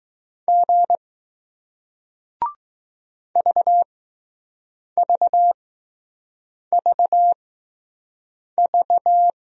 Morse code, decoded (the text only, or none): G VVVV